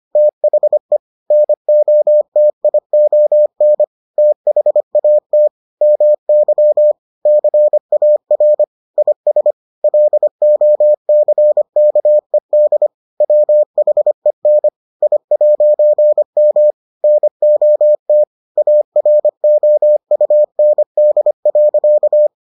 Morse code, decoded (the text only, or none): THE NOTION THAT MY CAR IS LOCKED WHEN I'M NOT AROUND.